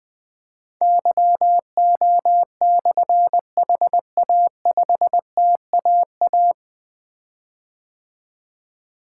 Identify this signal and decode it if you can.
Morse code, heard as a keyed tone, about 20 wpm, 700 Hz. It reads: YO/HA5TAA
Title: Difficult Call Signs 20wpm